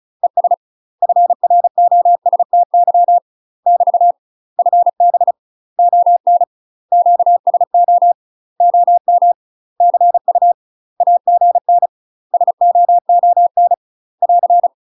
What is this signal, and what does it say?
Morse code, heard as a keyed tone, about 36 wpm, 700 Hz. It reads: ES FROSTY <BT> FB ON QSO OM CU AGN SOON <AR>
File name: UK MASTER QSOs TRAINING / 35WPM / UK MASTER QSOs 0009